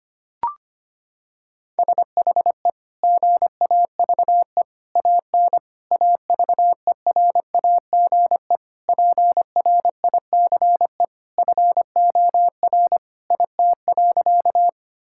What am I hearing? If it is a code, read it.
Morse code, heard as a keyed tone, about 25 wpm, 700 Hz. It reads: SHE GAVE AN AVERAGE PRICE FOR IT.
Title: Sentences from Top 1000 Words - Review Part 6 - 25wpm